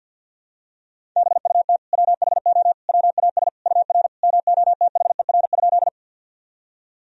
Morse code, heard as a keyed tone, about 52 wpm, 700 Hz. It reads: BUT WHO WAS UR MOTHER?
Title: Sentences from Top 200 Words and Abbreviations 50wpm